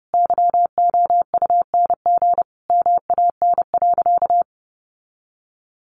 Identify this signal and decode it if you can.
Morse code, heard as a keyed tone, about 30 wpm, 700 Hz. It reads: YOUNG MAN.
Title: Sentences from Top 1300 Words 30wpm